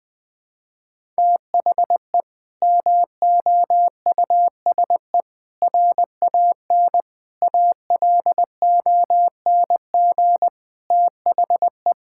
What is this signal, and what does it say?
Morse code, heard as a keyed tone, about 20 wpm, 700 Hz. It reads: THE MOUSE RAN ALONG THE
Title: Sentences from Top 1000 Words - Review Part 5 - 20wpm